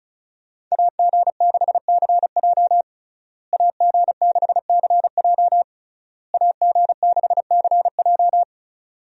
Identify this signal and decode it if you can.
Morse code, heard as a keyed tone, about 34 wpm, 700 Hz. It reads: AG6CJ AG6CJ AG6CJ
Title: USA Call Signs - 5 Characters - Repeated 3x - 35wpm